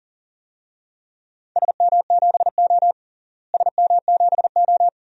Morse code, heard as a keyed tone, about 40 wpm, 700 Hz. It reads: SM7O SM7O
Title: DXCC-100 Call Signs - 4 Characters - Repeated 3x - 40wpm